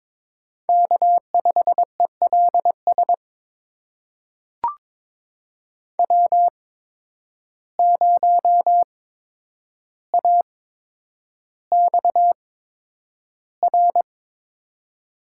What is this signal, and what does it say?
Morse code, heard as a keyed tone, about 22 wpm, spaced as for 9 wpm, 700 Hz. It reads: K5ELS W0AXR
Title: USA Call Signs - 5 Characters - 8x Character Spacing - 22wpm